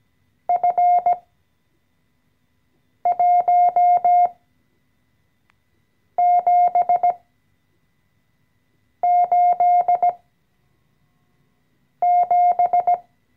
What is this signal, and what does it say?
Morse code, heard as a keyed tone, about 17 wpm, spaced as for 4 wpm, 710 Hz. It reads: F1787
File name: CW-Online-Kurs / Anfänger / Lektion 014